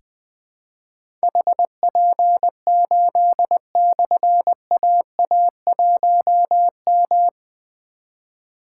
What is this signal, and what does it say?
Morse code, heard as a keyed tone, about 20 wpm, 700 Hz. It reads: HP8/AA1M
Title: Difficult Call Signs 20wpm